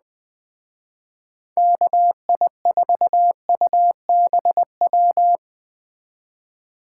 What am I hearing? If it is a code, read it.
Morse code, heard as a keyed tone, about 20 wpm, 700 Hz. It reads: KI4UBW